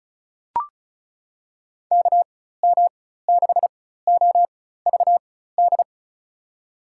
Morse code, heard as a keyed tone, about 36 wpm, spaced as for 16 wpm, 700 Hz. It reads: KM6OVD
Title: USA Call Signs - 4x Character Spacing - 35wpm